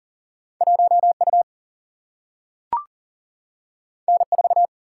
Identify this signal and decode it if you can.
Morse code, heard as a keyed tone, about 39 wpm, 700 Hz. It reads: 1 U N 4
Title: Single-Letter Single-Number 40wpm